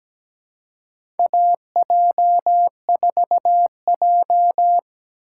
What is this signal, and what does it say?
Morse code, heard as a keyed tone, about 17 wpm, 700 Hz. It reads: AJ4J
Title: USA Call Signs - 4 Characters - 2x Character Spacing - 17wpm